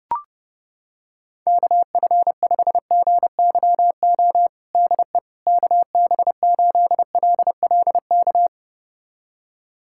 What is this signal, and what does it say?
Morse code, heard as a keyed tone, about 30 wpm, 700 Hz. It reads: KF5GYO DE KB8LLK
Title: USA Callsign DE Callsign 30wpm